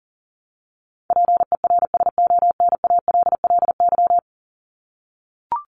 Morse code, heard as a keyed tone, about 40 wpm, 700 Hz. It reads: PERSONALLY
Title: Top 1251-1500 Words 40wpm